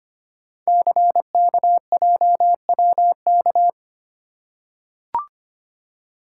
Morse code, heard as a keyed tone, about 25 wpm, 700 Hz. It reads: CKJWK
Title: Code Groups - 5 Letters - 25wpm